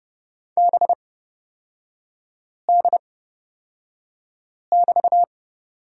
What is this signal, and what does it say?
Morse code, heard as a keyed tone, about 31 wpm, spaced as for 5 wpm, 700 Hz. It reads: BD<BT>
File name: Lesson 36e - BT - Three Characters 30wpm